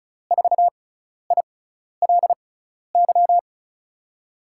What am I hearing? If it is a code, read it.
Morse code, heard as a keyed tone, about 36 wpm, spaced as for 12 wpm, 700 Hz. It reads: 4ILY
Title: USA Call Signs - 6 Characters - 6x Character Spacing - 35wpm